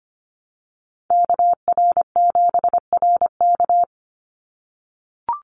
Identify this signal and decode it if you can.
Morse code, heard as a keyed tone, about 25 wpm, 700 Hz. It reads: KR7RK
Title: NA SOTA Chaser Callsigns 25wpm